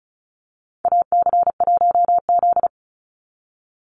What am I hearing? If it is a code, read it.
Morse code, heard as a keyed tone, about 34 wpm, 700 Hz. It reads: AC1Z
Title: NA SOTA Chaser Callsigns 35wpm